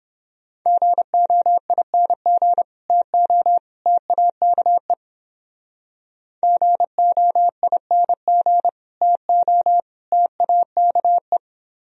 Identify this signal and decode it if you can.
Morse code, heard as a keyed tone, about 28 wpm, 700 Hz. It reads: GOING TO TAKE GOING TO TAKE